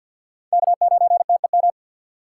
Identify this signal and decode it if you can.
Morse code, heard as a keyed tone, about 68 wpm, spaced as for 58 wpm, 700 Hz. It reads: K9TEM
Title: Speed Racing - USA 1x3 Call Signs 50wpm